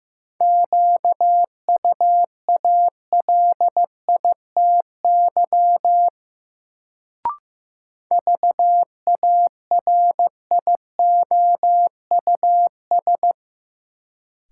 Morse code, heard as a keyed tone, about 15 wpm, 700 Hz. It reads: QUALITY VARIOUS
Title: Sentences from Top 1000 Words - Review Part 4 - New Word Drill - 15wpm